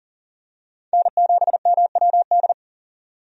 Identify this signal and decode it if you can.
Morse code, heard as a keyed tone, about 40 wpm, 700 Hz. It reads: N7MWD
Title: USA Call Signs - 5 Characters - 40wpm